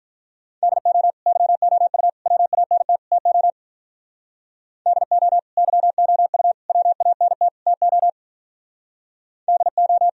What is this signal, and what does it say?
Morse code, heard as a keyed tone, about 43 wpm, 700 Hz. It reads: DO YOU WANT TO DO YOU WANT TO DO
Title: Speed-Racing - Sets of 4 Words 35wpm